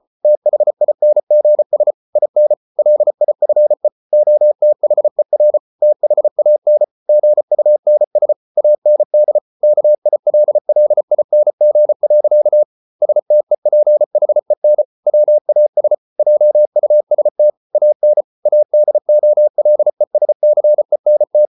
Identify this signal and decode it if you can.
Morse code, heard as a keyed tone, about 34 wpm, 600 Hz. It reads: THINGS IN LIFE OTHER THAN GUNS AND KILLING. STEPHEN WAS JUST AN ADOLESCENT